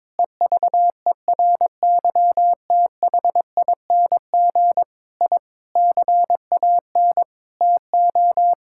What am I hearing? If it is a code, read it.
Morse code, heard as a keyed tone, about 22 wpm, 700 Hz. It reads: EVERYTHING I CAN TO
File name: practice / 22wpm / Sentences with 8-Word Phrases - 22wpm